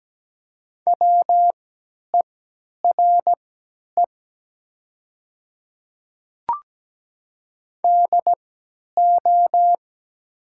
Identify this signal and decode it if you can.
Morse code, heard as a keyed tone, about 17 wpm, spaced as for 10 wpm, 700 Hz. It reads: WERE DO
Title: 3x Character Spacing - Top 300 Words - 17wpm